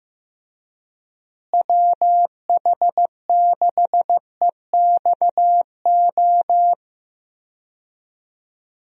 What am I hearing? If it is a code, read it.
Morse code, heard as a keyed tone, about 15 wpm, 700 Hz. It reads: WH6EXO
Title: USA Call Signs - 6 Characters - Repeated 3x - 15wpm